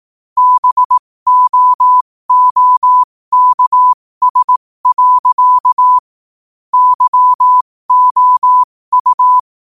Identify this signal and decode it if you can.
Morse code, heard as a keyed tone, about 18 wpm, spaced as for 15 wpm, 1000 Hz. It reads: BOOKS. YOU
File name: qotdfiles / 15wpm